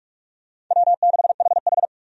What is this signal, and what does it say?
Morse code, heard as a keyed tone, about 46 wpm, 700 Hz. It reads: W6HH
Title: USA Call Signs - 4 Characters - Repeated 2x - 45wpm